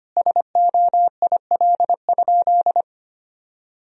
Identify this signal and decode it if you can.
Morse code, heard as a keyed tone, about 25 wpm, 700 Hz. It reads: SOIL?
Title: Sentences from Top 1000 Words - Review Part 4 - 25wpm